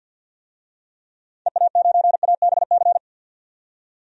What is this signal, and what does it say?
Morse code, heard as a keyed tone, about 50 wpm, 700 Hz. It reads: EA9ABC
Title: DXCC-100 Call Signs - 6 Characters - 50wpm